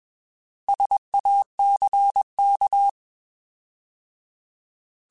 Morse code, heard as a keyed tone, about 21 wpm, 800 Hz. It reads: SACK
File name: Alphabetical / Alphabetical mp3s / Ss @20